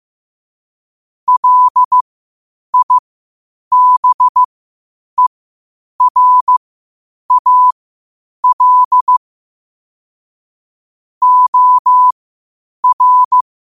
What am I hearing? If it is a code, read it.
Morse code, heard as a keyed tone, about 15 wpm, spaced as for 8 wpm, 1000 Hz. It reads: LIBERAL OR
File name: qotdfiles / 8wpm